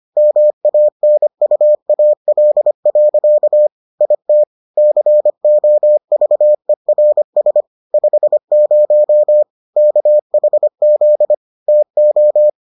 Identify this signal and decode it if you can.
Morse code, heard as a keyed tone, about 25 wpm, 600 Hz. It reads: MANUAL. IT COVERS 50 KHZ TO